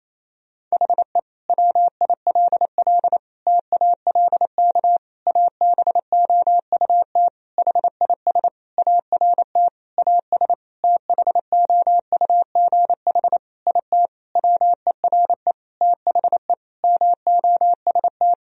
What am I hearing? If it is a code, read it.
Morse code, heard as a keyed tone, about 28 wpm, 700 Hz. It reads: HE WILL TALK ABOUT HIS ART AS THOUGH IT WERE THE MOST
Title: Sentences from Top 1000 Words - Review Part 3 - 28wpm